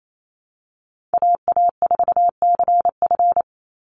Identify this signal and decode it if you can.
Morse code, heard as a keyed tone, about 28 wpm, 700 Hz. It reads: AA4CF
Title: Call Signs Repeated 2x - 28wpm